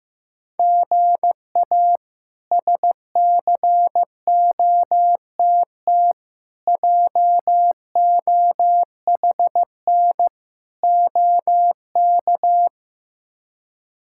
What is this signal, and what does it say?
Morse code, heard as a keyed tone, about 15 wpm, 700 Hz. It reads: GA SCOTT JOHN OK
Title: K1USN SST QSO Practice - 15wpm